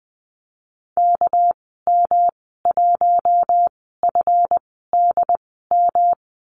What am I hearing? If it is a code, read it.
Morse code, heard as a keyed tone, about 20 wpm, spaced as for 15 wpm, 700 Hz. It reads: KM1FDM
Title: 200 Lesson 32g - Letter k - Call Signs 20wpm